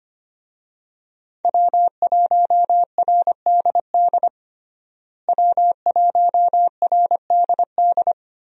Morse code, heard as a keyed tone, about 25 wpm, 700 Hz. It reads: W1RDD W1RDD
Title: USA Call Signs - 5 Characters - Repeated 3x - 25wpm